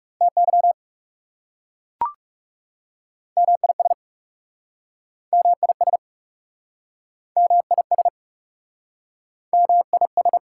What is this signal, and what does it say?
Morse code, heard as a keyed tone, about 40 wpm, 690 Hz. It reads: TY MIS MIS MIS MIS